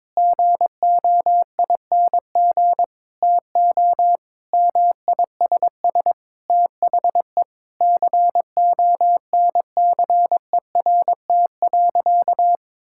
Morse code, heard as a keyed tone, about 22 wpm, 700 Hz. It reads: GOING TO MISS THE CONCERT.
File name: practice / 22wpm / Sentences with 8-Word Phrases - 22wpm